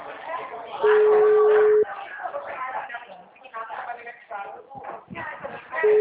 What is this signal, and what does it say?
Ringback tone (Europe, 425 Hz), with speech.